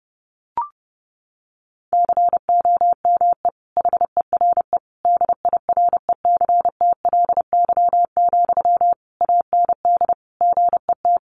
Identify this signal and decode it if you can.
Morse code, heard as a keyed tone, about 30 wpm, 700 Hz. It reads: COME HERE DIRECTLY, AND GET
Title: Sentences from Top 800 Words 30wpm